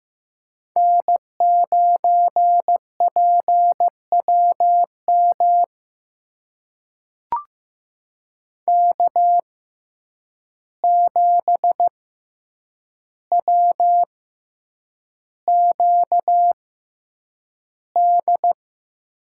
Morse code, heard as a keyed tone, about 15 wpm, spaced as for 8 wpm, 700 Hz. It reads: N9PWM K7WQD
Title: USA Call Signs - 5 Characters - 6x Character Spacing - 15wpm